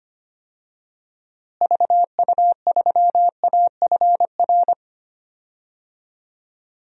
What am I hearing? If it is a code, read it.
Morse code, heard as a keyed tone, about 25 wpm, 700 Hz. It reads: VU3AFR
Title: DXCC-100 Call Signs - 6 Characters - Repeated 3x - 25wpm